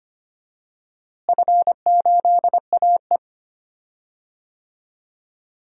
Morse code, heard as a keyed tone, about 25 wpm, 700 Hz. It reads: F8AE
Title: DXCC-100 Call Signs - 4 Characters - Repeated 2x - 25wpm